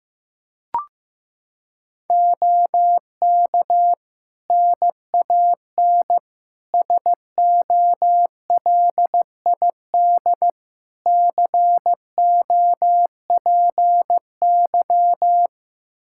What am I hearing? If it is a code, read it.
Morse code, heard as a keyed tone, about 15 wpm, 700 Hz. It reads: OK NAN SOLID COPY
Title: Practice QSO - Exchange 3 - More Info - 15wpm